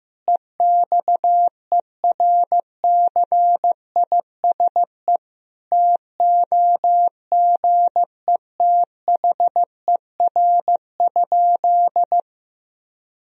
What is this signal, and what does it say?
Morse code, heard as a keyed tone, about 15 wpm, 700 Hz. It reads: EXERCISE TOGETHER?
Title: Sentences from Top 1000 Words - Review Part 4 - 15wpm